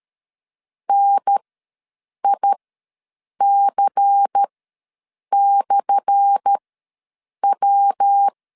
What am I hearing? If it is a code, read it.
Morse code, heard as a keyed tone, about 13 wpm, spaced as for 7 wpm, 800 Hz. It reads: NIC/W
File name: LEZIONI / 10 SIGNS / 10D COMA ws 12wpm 3618